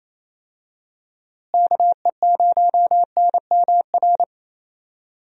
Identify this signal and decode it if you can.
Morse code, heard as a keyed tone, about 28 wpm, 700 Hz. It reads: KE0NMR
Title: USA Call Signs - 6 Characters - Repeated 2x - 28wpm